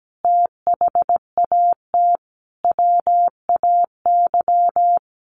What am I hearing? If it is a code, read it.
Morse code, heard as a keyed tone, about 17 wpm, 700 Hz. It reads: THAT WAY
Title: Sentences from Top 700 Words 17wpm